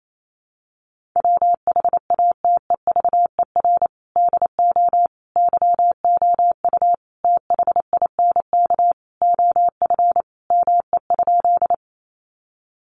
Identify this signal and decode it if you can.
Morse code, heard as a keyed tone, about 28 wpm, 700 Hz. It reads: WHATEVER DO YOU THINK OF ME?
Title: Sentences from Top 1100 Words 28wpm